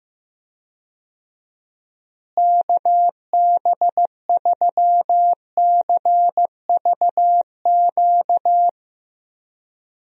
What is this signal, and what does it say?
Morse code, heard as a keyed tone, about 14 wpm, 700 Hz. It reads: KB3CVQ
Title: USA Call Signs - 6 Characters - Repeated 3x - 15wpm